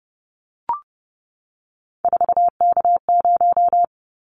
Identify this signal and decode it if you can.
Morse code, heard as a keyed tone, about 30 wpm, 700 Hz. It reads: 4K0
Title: Callsign Prefixes for Top 100 DXCC - 30wpm